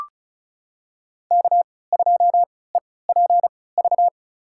Morse code, heard as a keyed tone, about 36 wpm, spaced as for 20 wpm, 700 Hz. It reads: K2EPV
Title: USA Call Signs - 5 Characters - 3x Character Spacing - 35wpm